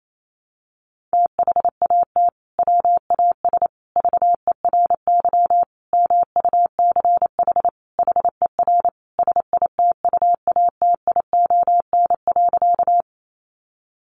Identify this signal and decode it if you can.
Morse code, heard as a keyed tone, about 28 wpm, 700 Hz. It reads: THAT WAS VERY MUCH HER SITUATION.
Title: Sentences from Top 1200 Words 28wpm